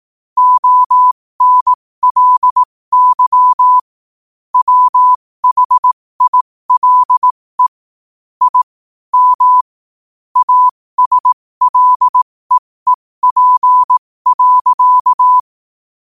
Morse code, heard as a keyed tone, about 18 wpm, spaced as for 15 wpm, 1000 Hz. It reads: ONLY WHILE I M ASLEEP.
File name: qotdfiles / 15wpm